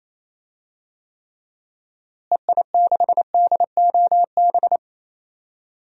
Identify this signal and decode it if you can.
Morse code, heard as a keyed tone, about 28 wpm, 700 Hz. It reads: EI6DOB
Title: DXCC-100 Call Signs - 6 Characters - Repeated 3x - 28wpm